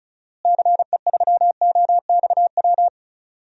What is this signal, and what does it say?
Morse code, heard as a keyed tone, about 34 wpm, 700 Hz. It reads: CE3OXW
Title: DXCC-100 Call Signs - 6 Characters - Repeated 3x - 35wpm